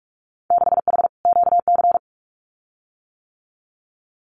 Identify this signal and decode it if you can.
Morse code, heard as a keyed tone, about 46 wpm, 700 Hz. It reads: 6H QC
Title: Winter Field Day Exchanges Encore 45wpm